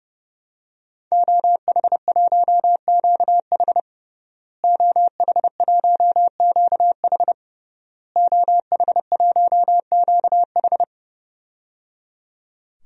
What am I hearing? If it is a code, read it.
Morse code, heard as a keyed tone, about 30 wpm, 700 Hz. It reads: OH1QH OH1QH OH1QH
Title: DXCC-100 Call Signs - 5 Characters - Repeated 3x - 30wpm